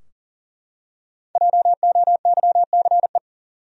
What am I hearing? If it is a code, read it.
Morse code, heard as a keyed tone, about 40 wpm, 700 Hz. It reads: JOYCE
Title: USA Common Names - 40wpm